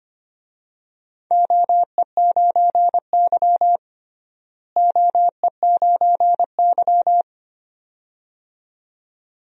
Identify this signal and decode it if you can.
Morse code, heard as a keyed tone, about 25 wpm, 700 Hz. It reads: OE9Y OE9Y
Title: DXCC-100 Call Signs - 4 Characters - Repeated 2x - 25wpm